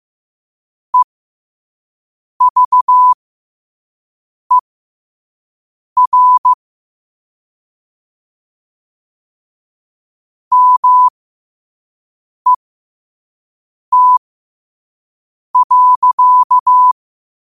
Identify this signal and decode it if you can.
Morse code, heard as a keyed tone, about 15 wpm, spaced as for 5 wpm, 1000 Hz. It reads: EVER MET.
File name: qotdfiles / 5wpm